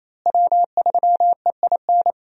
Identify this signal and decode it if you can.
Morse code, heard as a keyed tone, about 28 wpm, 700 Hz. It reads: W3EIN
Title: USA Call Signs - 5 Characters - Repeated 3x - 28wpm